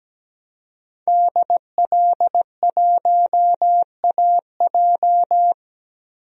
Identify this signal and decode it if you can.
Morse code, heard as a keyed tone, about 17 wpm, 700 Hz. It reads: DL1AJ
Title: DXCC-100 Call Signs - 5 Characters - Repeated 3x - 17wpm